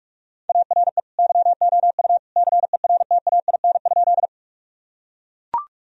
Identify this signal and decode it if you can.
Morse code, heard as a keyed tone, about 41 wpm, 700 Hz. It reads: ARE YOU CERTAIN?
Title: Sentences from Top 1000 Words - Review Part 2 - 45wpm